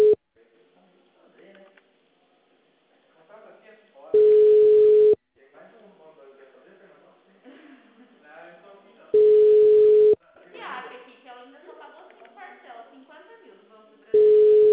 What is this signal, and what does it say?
Ringback tone (Europe, 425 Hz), with speech.